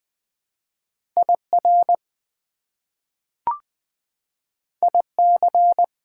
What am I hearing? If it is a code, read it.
Morse code, heard as a keyed tone, about 20 wpm, 700 Hz. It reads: IR IC
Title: Prefixes and Suffixes 20wpm